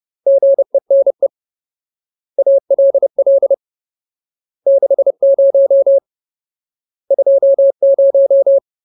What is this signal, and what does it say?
Morse code, heard as a keyed tone, about 30 wpm, 550 Hz. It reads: GENE ALL 60 20
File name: Ultimate Word List / 30WPM Ultimate